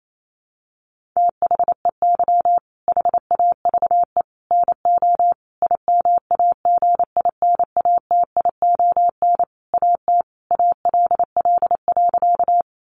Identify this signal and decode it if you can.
Morse code, heard as a keyed tone, about 28 wpm, 700 Hz. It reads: THEY HAVE NO IMAGINATION AT ALL.
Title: Sentences from Top 1300 Words 28wpm